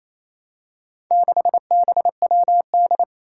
Morse code, heard as a keyed tone, about 28 wpm, 700 Hz. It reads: 6BWD
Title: Four Challenging Letter-Number Combinations 28wpm